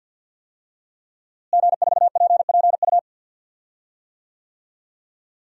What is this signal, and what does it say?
Morse code, heard as a keyed tone, about 53 wpm, 700 Hz. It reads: G4PPU
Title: DXCC-100 Call Signs - 5 Characters - Repeated 2x - 50wpm